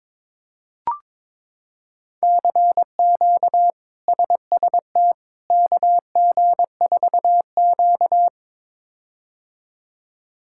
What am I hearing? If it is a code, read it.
Morse code, heard as a keyed tone, about 22 wpm, 700 Hz. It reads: CQ SST KG4Q
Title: K1USN SST QSO Practice - 22wpm